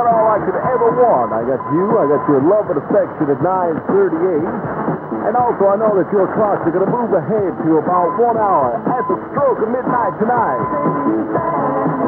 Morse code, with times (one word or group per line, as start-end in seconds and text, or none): none